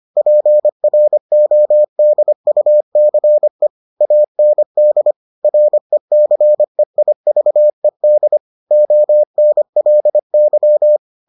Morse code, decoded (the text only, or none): PRODUCE AND RECEIVED ONLY